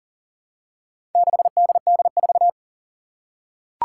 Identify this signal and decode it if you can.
Morse code, heard as a keyed tone, about 40 wpm, 700 Hz. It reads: 6DD4
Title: Four Challenging Letter-Number Combinations 40wpm